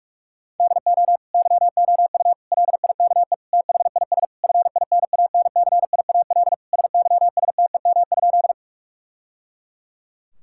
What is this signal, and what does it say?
Morse code, heard as a keyed tone, about 46 wpm, 700 Hz. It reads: DO YOU LIKE THIS FINANCIAL SYSTEM?
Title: Sentences from Top 1000 Words - Review Part 5 - 45wpm